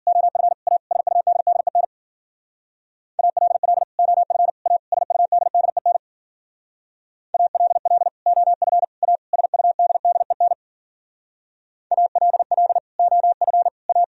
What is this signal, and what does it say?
Morse code, heard as a keyed tone, about 49 wpm, 700 Hz. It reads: OF A SUDDEN ALL OF A SUDDEN ALL OF A SUDDEN ALL OF A
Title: Speed-Racing - Sets of 4 Words 40wpm